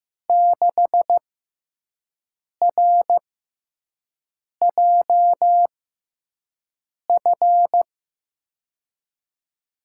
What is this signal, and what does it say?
Morse code, heard as a keyed tone, about 15 wpm, spaced as for 5 wpm, 700 Hz. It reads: 6RJF
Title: USA Call Signs - 6x Character Spacing - 15wpm